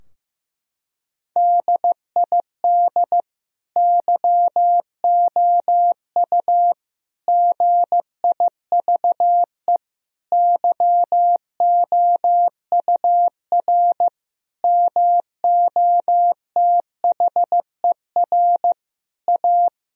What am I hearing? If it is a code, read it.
Morse code, heard as a keyed tone, about 15 wpm, 700 Hz. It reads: DID YOU GIVE YOUR MOTHER A
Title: Sentences from Top 1000 Words - Review Part 4 - 15wpm